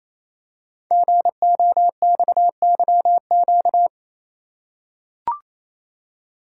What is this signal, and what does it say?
Morse code, heard as a keyed tone, about 27 wpm, 700 Hz. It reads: GOXYQ
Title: Code Groups - 5 Letter:Numbers - 28wpm